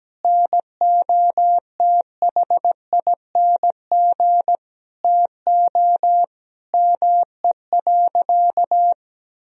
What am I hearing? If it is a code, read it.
Morse code, heard as a keyed tone, about 17 wpm, 700 Hz. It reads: NOTHING TO ME.
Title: Sentences from Top 1000 Words - Review Part 1 - 17wpm